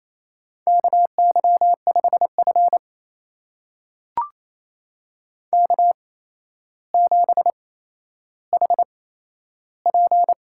Morse code, heard as a keyed tone, about 27 wpm, spaced as for 12 wpm, 700 Hz. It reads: KY5F K7HP
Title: USA Call Signs - 4 Characters - 8x Character Spacing - 28wpm